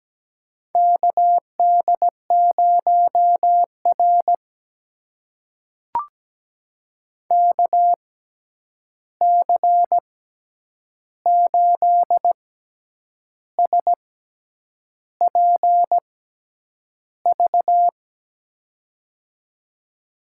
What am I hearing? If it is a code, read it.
Morse code, heard as a keyed tone, about 17 wpm, spaced as for 8 wpm, 700 Hz. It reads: KD0R KC8SPV
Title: USA Call Signs - 6x Character Spacing - 17wpm